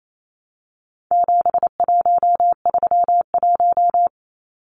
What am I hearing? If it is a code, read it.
Morse code, heard as a keyed tone, about 28 wpm, 700 Hz. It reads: 7131
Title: Numbers - 4-Digit 28wpm